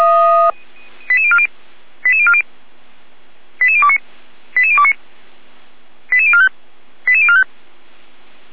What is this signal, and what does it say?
Five-tone selective call (ZVEI), heard as quick runs of five tones, tones of about 70 ms. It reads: 88040 88040 88029 88029 88045 88045